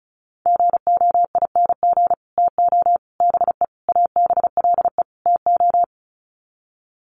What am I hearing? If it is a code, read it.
Morse code, heard as a keyed tone, about 34 wpm, 700 Hz. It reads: GOING TO BE ABLE TO
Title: Sets of 5 Words 35wpm